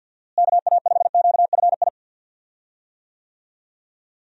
Morse code, heard as a keyed tone, about 50 wpm, 700 Hz. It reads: KA5QFI
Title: USA Call Signs - 6 Characters - Repeated 3x - 50wpm